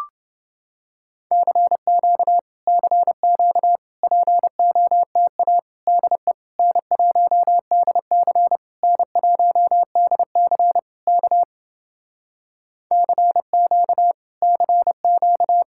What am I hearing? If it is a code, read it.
Morse code, heard as a keyed tone, about 29 wpm, 700 Hz. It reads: CQ CQ POTA DE N1DC N1DC K CQ CQ